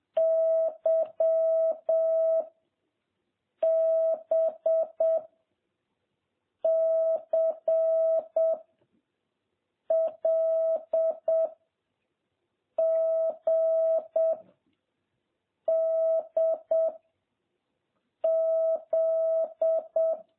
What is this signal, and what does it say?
Morse code, heard as a keyed tone, about 7 wpm, 660 Hz. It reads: Y B C L G D Z
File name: jh8cbh / 2021cw